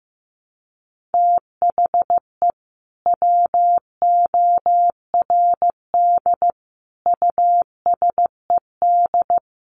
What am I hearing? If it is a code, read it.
Morse code, heard as a keyed tone, about 14 wpm, 700 Hz. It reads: THE WORD USED
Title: Sentences from Top 1100 Words 15wpm